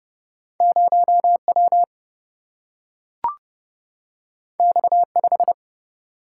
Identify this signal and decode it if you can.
Morse code, heard as a keyed tone, about 30 wpm, 700 Hz. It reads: 0 W X 5
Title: Single-Letter Single-Number 30wpm